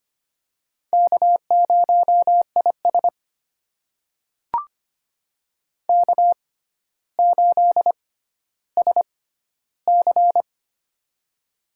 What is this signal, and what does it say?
Morse code, heard as a keyed tone, about 25 wpm, spaced as for 13 wpm, 700 Hz. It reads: K0IS K8SC
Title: USA Call Signs - 6x Character Spacing - 25wpm